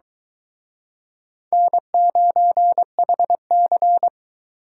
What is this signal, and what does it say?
Morse code, heard as a keyed tone, about 22 wpm, 700 Hz. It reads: N9HC